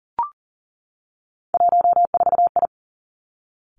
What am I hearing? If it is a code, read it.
Morse code, heard as a keyed tone, about 40 wpm, 700 Hz. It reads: 14I
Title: Three Challenging Letter-Number Combinations 40wpm